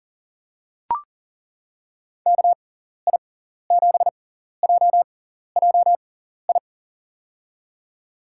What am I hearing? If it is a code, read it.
Morse code, heard as a keyed tone, about 40 wpm, spaced as for 14 wpm, 700 Hz. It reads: KI7JJI
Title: USA Call Signs - 6 Characters - 6x Character Spacing - 40wpm